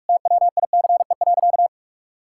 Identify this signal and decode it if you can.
Morse code, heard as a keyed tone, about 46 wpm, 700 Hz. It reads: TWICE.
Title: Sentences from Top 1600 Words 45wpm